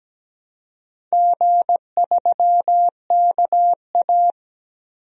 Morse code, heard as a keyed tone, about 17 wpm, 700 Hz. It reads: G3KA
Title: DXCC-100 Call Signs - 4 Characters - 17wpm